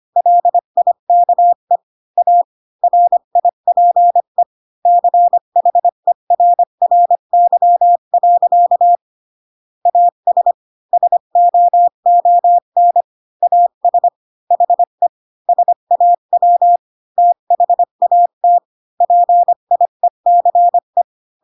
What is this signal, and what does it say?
Morse code, heard as a keyed tone, about 25 wpm, 700 Hz. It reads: LIKE A RIPE CHERRY. AS SOON AS HE SAW THAT PIECE
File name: Pinocchio / Pinocchio ch1 20 25 ik1hge